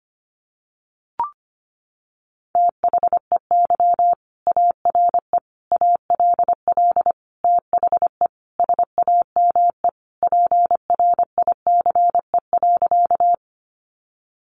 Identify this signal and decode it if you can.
Morse code, heard as a keyed tone, about 25 wpm, 700 Hz. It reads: THEY ARE ALL THE SAME PRICE.
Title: Sentences from Top 600 Words 25wpm